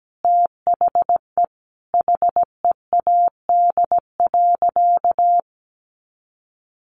Morse code, heard as a keyed tone, about 17 wpm, 700 Hz. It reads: THE HEAD.
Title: Sentences from Top 1200 Words 17wpm